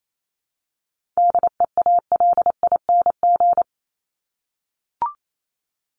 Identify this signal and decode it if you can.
Morse code, heard as a keyed tone, about 28 wpm, 700 Hz. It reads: DEALING
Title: ING Suffix - Top 100 Words with Base Words 28wpm